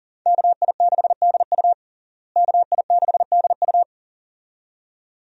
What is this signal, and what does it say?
Morse code, heard as a keyed tone, about 37 wpm, 700 Hz. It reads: KI6DU KI6DU
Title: USA Call Signs - 5 Characters - Repeated 2x - 40wpm